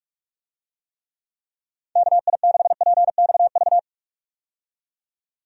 Morse code, heard as a keyed tone, about 43 wpm, 700 Hz. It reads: KI6PXV
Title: USA Call Signs - 6 Characters - 45wpm